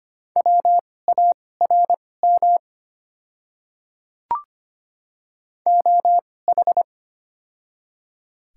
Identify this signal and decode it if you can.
Morse code, heard as a keyed tone, about 25 wpm, spaced as for 18 wpm, 700 Hz. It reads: WARM OH
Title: 2x Character Spacing - Top 500 Words - 25wpm